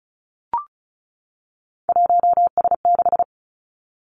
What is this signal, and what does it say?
Morse code, heard as a keyed tone, about 36 wpm, 700 Hz. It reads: 1S6
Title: Three Challenging Letter-Number Combinations 35wpm